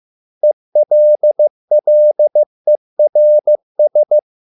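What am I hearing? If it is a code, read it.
Morse code, heard as a keyed tone, about 15 wpm, 600 Hz. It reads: ELLERS
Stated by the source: Morse code at a slow 15 wpm from CW med Gnister.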